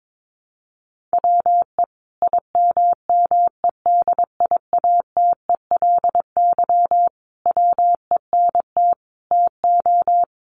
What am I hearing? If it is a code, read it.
Morse code, heard as a keyed tone, about 22 wpm, 700 Hz. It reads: WE IMMEDIATELY WENT TO
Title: Sentences from Top 1100 Words 22wpm